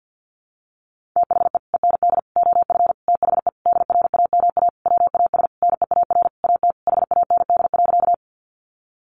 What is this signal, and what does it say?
Morse code, heard as a keyed tone, about 50 wpm, 700 Hz. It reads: THE END OF THE DRAMA WAS NEAR AT HAND.
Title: Sentences from Top 1500 Words 50wpm